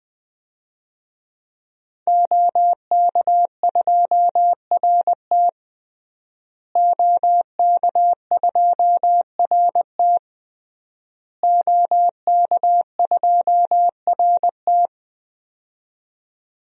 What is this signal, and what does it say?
Morse code, heard as a keyed tone, about 20 wpm, 700 Hz. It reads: OK2RT OK2RT OK2RT
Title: DXCC-100 Call Signs - 5 Characters - Repeated 3x - 20wpm